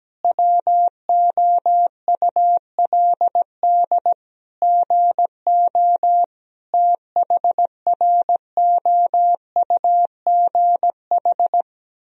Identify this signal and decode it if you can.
Morse code, heard as a keyed tone, about 17 wpm, 700 Hz. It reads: WOULD GO THROUGH
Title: Sentences from Top 1000 Words - Review Part 3 - 17wpm